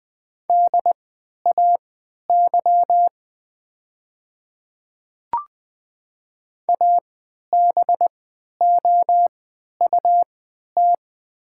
Morse code, heard as a keyed tone, about 20 wpm, spaced as for 11 wpm, 690 Hz. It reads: DAY ABOUT
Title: 3x Character Spacing - Top 100 Words - 20wpm